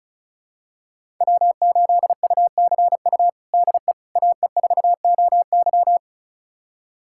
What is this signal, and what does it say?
Morse code, heard as a keyed tone, about 36 wpm, 700 Hz. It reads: W8UCU DE AE4OY
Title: USA Callsign DE Callsign 35wpm